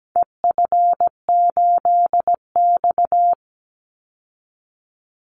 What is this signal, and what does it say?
Morse code, heard as a keyed tone, about 17 wpm, 700 Hz. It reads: EF8X
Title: Call Signs - 17wpm